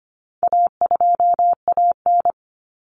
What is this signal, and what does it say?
Morse code, heard as a keyed tone, about 25 wpm, 700 Hz. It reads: A2AN
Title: Common Cut Numbers - 4-Digit 25wpm